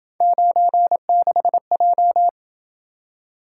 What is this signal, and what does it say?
Morse code, heard as a keyed tone, about 27 wpm, 700 Hz. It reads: 96J